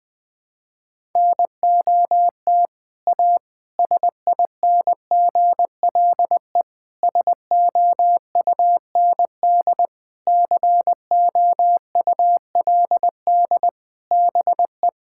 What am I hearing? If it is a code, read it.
Morse code, heard as a keyed tone, about 20 wpm, 700 Hz. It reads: NOT A SINGLE SOUND COULD BE
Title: Sentences from Top 1000 Words - Review Part 5 - 20wpm